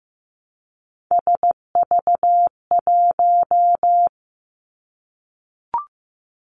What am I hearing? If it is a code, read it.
Morse code, heard as a keyed tone, about 15 wpm, 700 Hz. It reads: SV1
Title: Callsign Prefixes for Top 50 DXCC - 15wpm